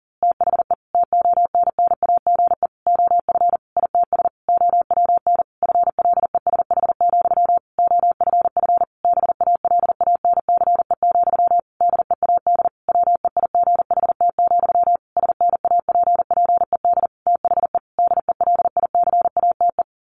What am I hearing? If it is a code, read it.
Morse code, heard as a keyed tone, about 40 wpm, 700 Hz. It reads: THE TONNAGE OF ITS OWN FLESH, OFF BALANCE, DEAD WEIGHT, SNAPPED THE DELICATE
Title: A Sound Of Thunder 40wpm